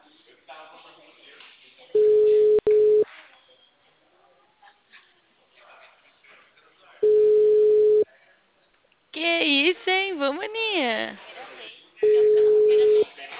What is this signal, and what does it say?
Ringback tone (Europe, 425 Hz), with speech.
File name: out- -2004- - - .64725